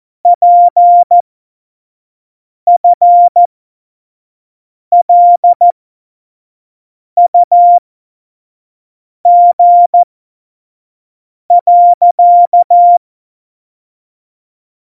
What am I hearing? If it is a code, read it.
Morse code, heard as a keyed tone, about 14 wpm, spaced as for 5 wpm, 700 Hz. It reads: PFLUG.
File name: Morsekurs / Witze / Witz 27. Oktober 2020, 25 F70